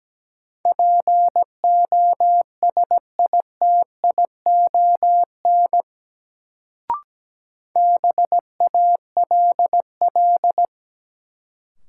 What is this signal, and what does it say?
Morse code, heard as a keyed tone, about 17 wpm, 700 Hz. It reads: POSITION BALL
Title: Sentences from Top 900 Words - New Word Drill - 17wpm